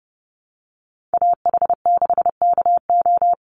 Morse code, heard as a keyed tone, about 30 wpm, 700 Hz. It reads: AH6KO
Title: Call Signs Repeated 3x - 30wpm